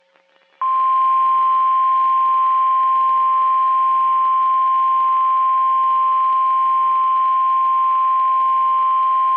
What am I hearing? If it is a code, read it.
Emergency Alert System (SAME) tones (the 1050 Hz attention tone). It is heard from 1.2 to 8.8 s.
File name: eas messages / EAS WXM31 Elmira